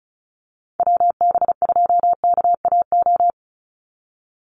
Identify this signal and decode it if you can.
Morse code, heard as a keyed tone, about 34 wpm, 700 Hz. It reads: WB2KAO
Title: Call Signs Repeated 3x - 35wpm